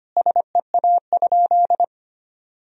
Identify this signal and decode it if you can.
Morse code, heard as a keyed tone, about 25 wpm, 700 Hz. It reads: SEA?
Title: Sentences from Top 1000 Words - Review Part 3 - 25wpm